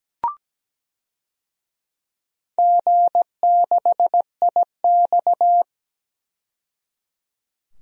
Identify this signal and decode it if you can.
Morse code, heard as a keyed tone, about 17 wpm, 700 Hz. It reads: G6IX
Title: DXCC-100 Call Signs - 4 Characters - 17wpm